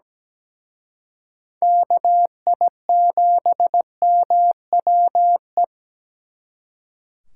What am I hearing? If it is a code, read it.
Morse code, heard as a keyed tone, about 20 wpm, 700 Hz. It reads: KI7MWE
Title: Speed Racing - USA 2x3 Call Signs 17wpm